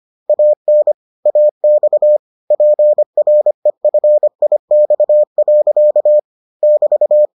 Morse code, heard as a keyed tone, about 25 wpm, 600 Hz. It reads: AN AX PREFIX. <BT>